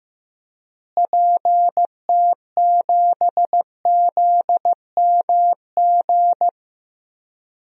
Morse code, heard as a keyed tone, about 15 wpm, 700 Hz. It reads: PT7ZMG
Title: DXCC-100 Call Signs - 6 Characters - Repeated 2x - 15wpm